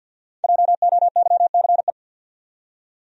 Morse code, heard as a keyed tone, about 52 wpm, 700 Hz. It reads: JOYCE
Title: USA Common Names - 50wpm